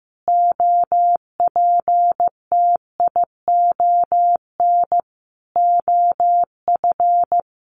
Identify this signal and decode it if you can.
Morse code, heard as a keyed tone, about 15 wpm, 700 Hz. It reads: OPTION OF
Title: Sentences from Top 1300 Words 15wpm